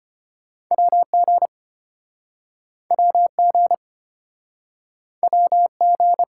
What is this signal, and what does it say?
Morse code, heard as a keyed tone, about 33 wpm, 690 Hz. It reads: WG WG WG